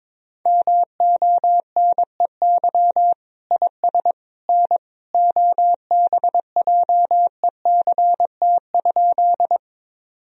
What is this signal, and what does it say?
Morse code, heard as a keyed tone, about 22 wpm, 700 Hz. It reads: MONEY IS N OBJECT?
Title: Sentences from Top 500 Words and Abbreviations 22wpm